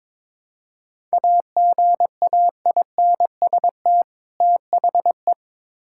Morse code, heard as a keyed tone, about 21 wpm, 700 Hz. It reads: AGAINST THE
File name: Build A Phrase - 2-Word Phrases - 22wpm